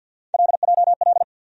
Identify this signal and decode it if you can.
Morse code, heard as a keyed tone, about 53 wpm, 700 Hz. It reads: LJL
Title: USA Call Sign Postfixes - 3 Letters 50wpm